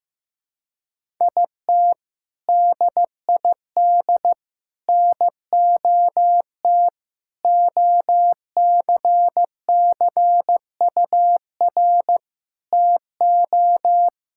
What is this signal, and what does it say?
Morse code, heard as a keyed tone, about 15 wpm, 700 Hz. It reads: IT DID NOT OCCUR TO
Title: Sentences from Top 1000 Words - Review Part 6 - 15wpm